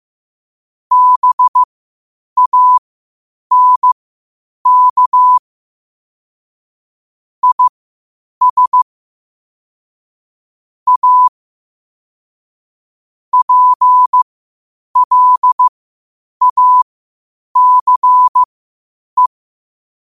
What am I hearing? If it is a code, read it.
Morse code, heard as a keyed tone, about 15 wpm, spaced as for 8 wpm, 1000 Hz. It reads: BANK IS A PLACE
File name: qotdfiles / 8wpm